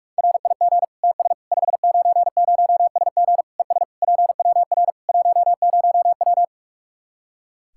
Morse code, heard as a keyed tone, about 42 wpm, 700 Hz. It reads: RIG TS 590SG ES PWR 10W
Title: Practice QSO - Exchange 2 - Info - 45wpm